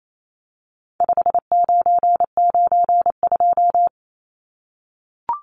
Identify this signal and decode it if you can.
Morse code, heard as a keyed tone, about 28 wpm, 700 Hz. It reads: 5992
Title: Numbers - 4-Digit 28wpm